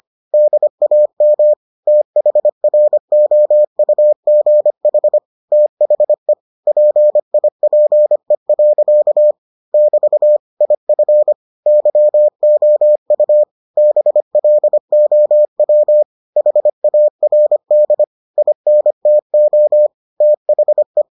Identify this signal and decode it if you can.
Morse code, heard as a keyed tone, about 25 wpm, 600 Hz. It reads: DAM THROUGH THE PIPE. <BT> IF YOU BLOW HARD INTO THE